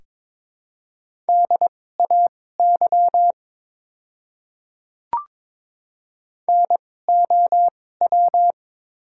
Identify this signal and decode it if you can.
Morse code, heard as a keyed tone, about 22 wpm, spaced as for 16 wpm, 700 Hz. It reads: DAY NOW
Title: 2x Character Spacing - Top 500 Words - 22wpm